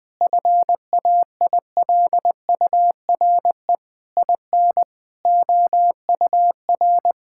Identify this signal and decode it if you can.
Morse code, heard as a keyed tone, about 20 wpm, 700 Hz. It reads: FAILURE IN OUR
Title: Sentences from Top 1000 Words - Review Part 6 - 20wpm